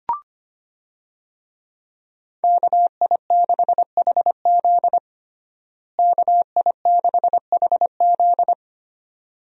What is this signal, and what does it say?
Morse code, heard as a keyed tone, about 25 wpm, 700 Hz. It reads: KI6HZ KI6HZ
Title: USA Call Signs - 5 Characters - Repeated 3x - 25wpm